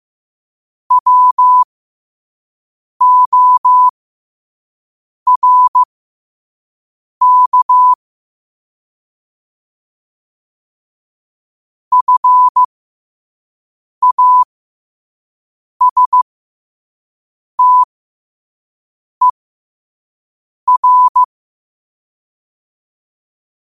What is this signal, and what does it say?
Morse code, heard as a keyed tone, about 15 wpm, spaced as for 5 wpm, 1000 Hz. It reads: WORK FASTER
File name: qotdfiles / 5wpm